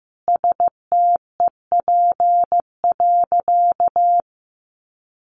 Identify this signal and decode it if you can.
Morse code, heard as a keyed tone, about 15 wpm, 700 Hz. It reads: STEP.
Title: Sentences from Top 1400 Words 15wpm